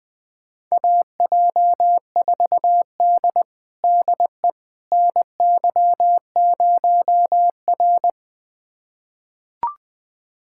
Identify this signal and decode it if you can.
Morse code, heard as a keyed tone, about 20 wpm, 700 Hz. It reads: AJ4D DE NY0R
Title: USA Callsign DE Callsign - 4 Characters - 20wpm